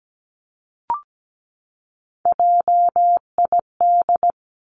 Morse code, heard as a keyed tone, about 17 wpm, 700 Hz. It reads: JID
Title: Three Challenging Letter-Number Combinations 17wpm